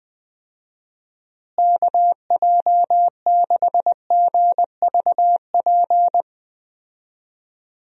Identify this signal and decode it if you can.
Morse code, heard as a keyed tone, about 20 wpm, 700 Hz. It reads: KJ6GVP
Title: USA Call Signs - 6 Characters - Repeated 3x - 20wpm